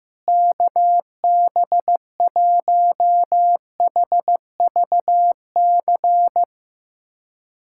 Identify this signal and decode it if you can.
Morse code, heard as a keyed tone, about 15 wpm, 700 Hz. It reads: KB1HVC
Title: USA Callsign DE Callsign 15wpm